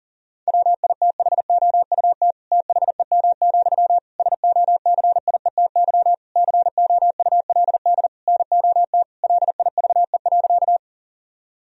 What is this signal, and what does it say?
Morse code, heard as a keyed tone, about 40 wpm, 700 Hz. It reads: WITHOUT THEM, SOCIETY COULD NOT LIVE.
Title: Sentences from Top 1000 Words - Review Part 4 - 40wpm